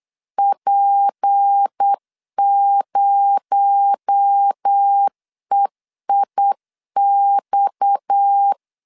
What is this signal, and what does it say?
Morse code, heard as a keyed tone, about 8 wpm, 800 Hz. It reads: P0EIX
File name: LEZIONI / 7 GF9Z / 7 GF9 8wpm 3545